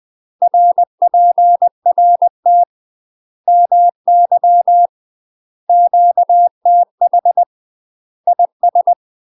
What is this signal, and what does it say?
Morse code, heard as a keyed tone, about 20 wpm, 700 Hz. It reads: RPRT MY QTH IS
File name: UK MASTER QSOs TRAINING / 20WPM / UK MASTER QSOs 0043